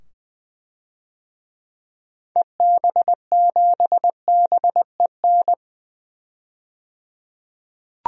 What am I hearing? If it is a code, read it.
Morse code, heard as a keyed tone, about 20 wpm, 700 Hz. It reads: EB7BEN
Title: DXCC-100 Call Signs - 6 Characters - 20wpm